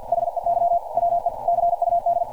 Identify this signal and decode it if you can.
Morse code, heard as a keyed tone, about 51 wpm, 690 Hz. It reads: HEY WE CUD
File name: archived / CRSnet